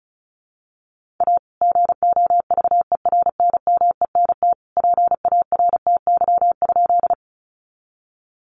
Morse code, heard as a keyed tone, about 34 wpm, 700 Hz. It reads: A GOVERNMENT PARTY?
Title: Sentences from Top 1400 Words 35wpm